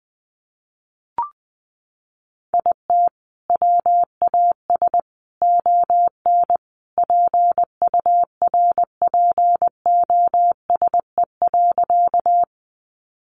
Morse code, heard as a keyed tone, about 20 wpm, 700 Hz. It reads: IT WAS ON PURPOSE.
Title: Sentences from Top 800 Words 20wpm